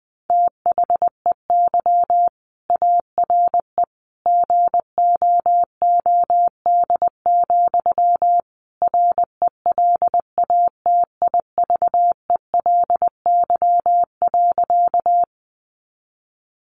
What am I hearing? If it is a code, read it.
Morse code, heard as a keyed tone, about 19 wpm, 700 Hz. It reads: THEY ARE GOOD, RELATIVELY.
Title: Sentences from Top 1100 Words 20wpm